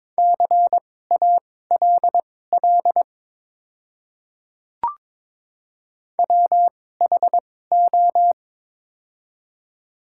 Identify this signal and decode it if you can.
Morse code, heard as a keyed tone, about 22 wpm, spaced as for 16 wpm, 700 Hz. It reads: CALL WHO
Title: 2x Character Spacing - Top 500 Words - 22wpm